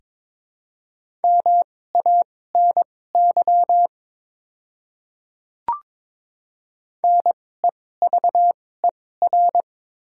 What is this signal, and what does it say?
Morse code, heard as a keyed tone, about 22 wpm, spaced as for 16 wpm, 700 Hz. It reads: MANY NEVER
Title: 2x Character Spacing - Top 300 Words - 22wpm